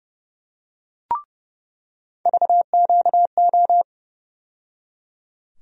Morse code, heard as a keyed tone, about 30 wpm, 700 Hz. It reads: VQO
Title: USA Call Sign Postfixes - 3 Letters 30wpm